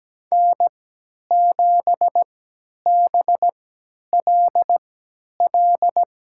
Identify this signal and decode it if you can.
Morse code, heard as a keyed tone, about 17 wpm, spaced as for 10 wpm, 700 Hz. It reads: N7BLL
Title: USA Call Signs - 5 Characters - 3x Character Spacing - 17wpm